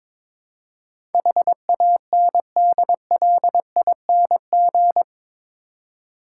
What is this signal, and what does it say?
Morse code, heard as a keyed tone, about 22 wpm, 700 Hz. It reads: HANDLING
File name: practice / 22wpm / Eight Letter Words - 22wpm